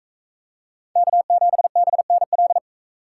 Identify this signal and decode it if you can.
Morse code, heard as a keyed tone, about 38 wpm, 700 Hz. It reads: K7BNL